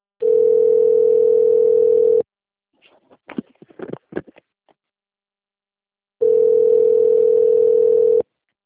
Ringback tone (North America), 440 + 480 Hz.